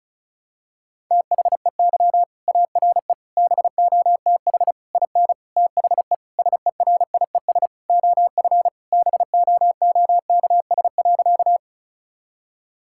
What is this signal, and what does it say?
Morse code, heard as a keyed tone, about 36 wpm, 700 Hz. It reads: THEY ARE BOTH IN THE SERIES OF BOOKS.
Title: Sentences from Top 1000 Words - Review Part 5 - 35wpm